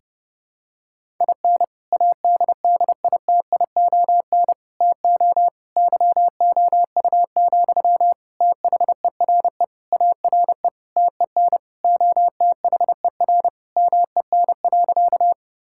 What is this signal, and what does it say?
Morse code, heard as a keyed tone, about 32 wpm, 700 Hz. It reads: IN ADDITION TO YOU, THERE ARE TEN OTHER MEN.
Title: Sentences from Top 1000 Words - Review Part 6 - 30wpm